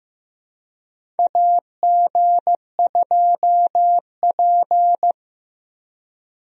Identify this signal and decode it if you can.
Morse code, heard as a keyed tone, about 15 wpm, 700 Hz. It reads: AG2P
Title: USA Call Signs - 4 Characters - 15wpm